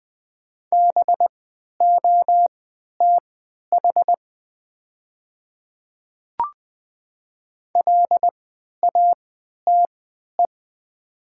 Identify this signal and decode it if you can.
Morse code, heard as a keyed tone, about 20 wpm, spaced as for 11 wpm, 700 Hz. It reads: BOTH LATE
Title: 3x Character Spacing - Top 500 Words - 20wpm